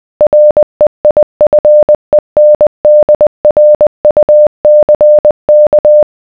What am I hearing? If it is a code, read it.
Morse code, heard as a keyed tone, about 20 wpm, 600 Hz. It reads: REIFENDRUCK